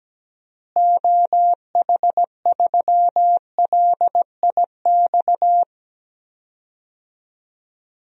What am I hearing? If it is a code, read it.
Morse code, heard as a keyed tone, about 17 wpm, 700 Hz. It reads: OH3LIX
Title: DXCC-100 Call Signs - 6 Characters - Repeated 2x - 17wpm